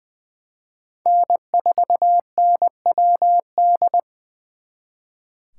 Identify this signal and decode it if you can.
Morse code, heard as a keyed tone, about 25 wpm, 700 Hz. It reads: N4NWD
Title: Speed Racing - USA Call Signs 20wpm